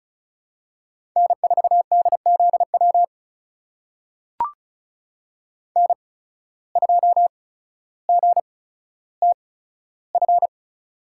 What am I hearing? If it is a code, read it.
Morse code, heard as a keyed tone, about 36 wpm, spaced as for 15 wpm, 700 Hz. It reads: N4DZW N2GTF
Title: USA Call Signs - 5 Characters - 8x Character Spacing - 35wpm